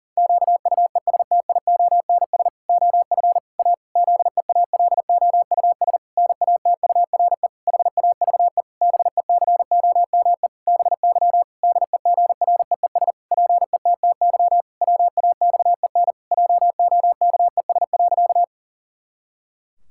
Morse code, heard as a keyed tone, about 40 wpm, 700 Hz. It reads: QUESTIONS OF A ZEALOUS NATURE HAVE BECOME BY DEGREES PETTY WAXEN JOKES.
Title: Pangram Challenge v2 - 40wpm